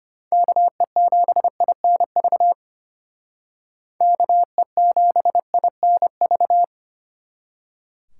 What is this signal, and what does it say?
Morse code, heard as a keyed tone, about 33 wpm, 700 Hz. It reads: KE7INV KE7INV